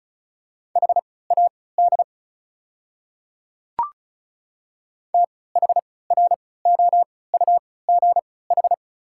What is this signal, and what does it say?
Morse code, heard as a keyed tone, about 36 wpm, spaced as for 20 wpm, 700 Hz. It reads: HAD THROUGH
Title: 3x Character Spacing - Top 200 Words - 35wpm